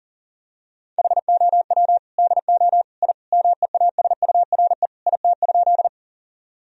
Morse code, heard as a keyed tone, about 40 wpm, 700 Hz. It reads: HOW DO I MEASURE IT?
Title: Sentences from Top 1000 Words - Review Part 2 - 40wpm